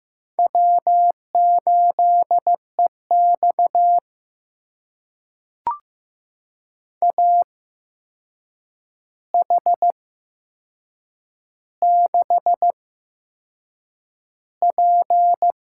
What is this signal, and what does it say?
Morse code, heard as a keyed tone, about 15 wpm, spaced as for 6 wpm, 700 Hz. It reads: W8EX AH6P
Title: USA Call Signs - 4 Characters - 8x Character Spacing - 15wpm